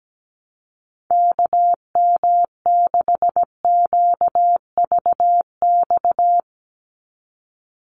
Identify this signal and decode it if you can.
Morse code, heard as a keyed tone, about 17 wpm, 700 Hz. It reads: KM6QVX
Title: USA Call Signs 17wpm